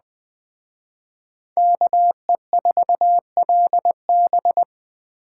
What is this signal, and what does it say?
Morse code, heard as a keyed tone, about 20 wpm, 700 Hz. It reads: KE4LB
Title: Speed Racing - USA Call Signs 17wpm